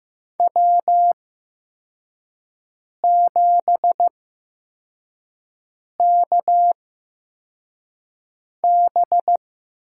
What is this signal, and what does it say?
Morse code, heard as a keyed tone, about 15 wpm, spaced as for 4 wpm, 700 Hz. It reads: W7KB
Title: USA Call Signs - 8x Character Spacing - 15wpm